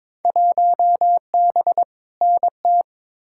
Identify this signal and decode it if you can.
Morse code, heard as a keyed tone, about 22 wpm, 700 Hz. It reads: 1B NT
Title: Field Day Exchanges Encore - 22wpm